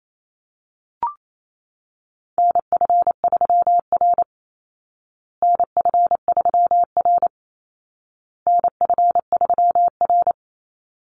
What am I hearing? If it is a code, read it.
Morse code, heard as a keyed tone, about 28 wpm, 700 Hz. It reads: NF3R NF3R NF3R
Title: Call Signs Repeated 2x - 28wpm